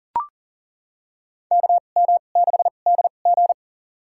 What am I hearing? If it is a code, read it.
Morse code, heard as a keyed tone, about 40 wpm, spaced as for 29 wpm, 700 Hz. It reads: KM6DG
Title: USA Call Signs - 5 Characters - 2x Character Spacing - 40wpm